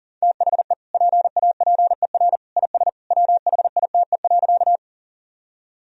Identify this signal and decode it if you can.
Morse code, heard as a keyed tone, about 40 wpm, 700 Hz. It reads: THE PAPER IS WHITE.
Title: Sentences from Top 1000 Words - Review Part 1 - 40wpm